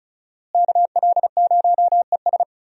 Morse code, heard as a keyed tone, about 36 wpm, 700 Hz. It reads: KL0ES
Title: USA Call Signs - 5 Characters - Repeated 2x - 35wpm